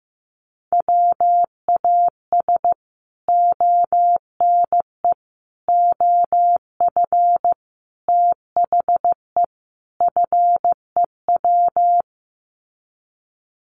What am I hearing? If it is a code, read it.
Morse code, heard as a keyed tone, about 15 wpm, 700 Hz. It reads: WAS ONE OF THE FEW
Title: Sets of 5 Words Encore 15wpm